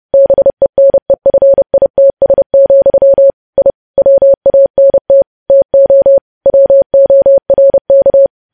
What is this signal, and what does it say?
Morse code, heard as a keyed tone, about 30 wpm, 550 Hz. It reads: BENEFITS, I WANT TO WORK
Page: News Headlines in Morse Code at 30 WPM